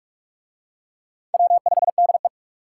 Morse code, heard as a keyed tone, about 43 wpm, 700 Hz. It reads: W5DE